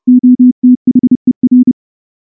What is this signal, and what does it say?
Morse code, heard as a keyed tone, about 28 wpm, 250 Hz. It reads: OTHER
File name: morse-training-30WPM